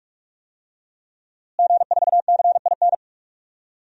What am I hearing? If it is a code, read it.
Morse code, heard as a keyed tone, about 46 wpm, 700 Hz. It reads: G4CIN
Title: DXCC-100 Call Signs - 5 Characters - Repeated 2x - 45wpm